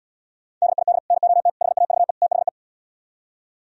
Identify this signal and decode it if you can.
Morse code, heard as a keyed tone, about 77 wpm, 700 Hz. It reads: DID NOT HAVE THE